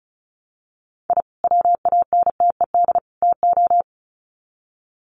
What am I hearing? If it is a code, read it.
Morse code, heard as a keyed tone, about 34 wpm, 700 Hz. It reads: I WANTED TO
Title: Sets of 3 Words 35wpm